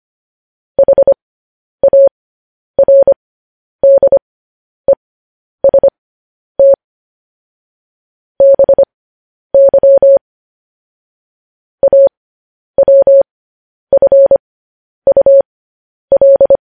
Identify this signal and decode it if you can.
Morse code, heard as a keyed tone, about 25 wpm, spaced as for 10 wpm, 550 Hz. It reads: HARDEST BY AWFUL
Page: news-headlines-in-morse-code-at-10-wpm